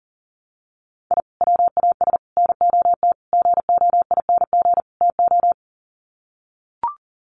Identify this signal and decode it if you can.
Morse code, heard as a keyed tone, about 37 wpm, 700 Hz. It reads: I WAS NOT GOING TO
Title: Sets of 5 Words 40wpm